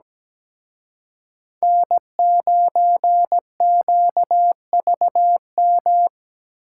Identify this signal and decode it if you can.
Morse code, heard as a keyed tone, about 20 wpm, 700 Hz. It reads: N9QVM
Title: Speed Racing - USA Call Signs 17wpm